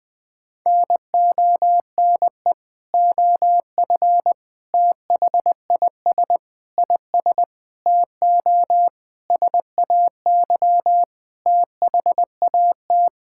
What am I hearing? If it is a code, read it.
Morse code, heard as a keyed tone, about 21 wpm, 700 Hz. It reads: NONE OF THIS IS TO SAY THAT
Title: Sentences with 7-Word Phrases - 20wpm